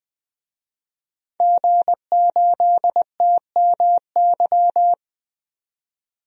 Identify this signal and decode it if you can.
Morse code, heard as a keyed tone, about 20 wpm, 700 Hz. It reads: G8TMY
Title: DXCC-100 Call Signs - 5 Characters - 20wpm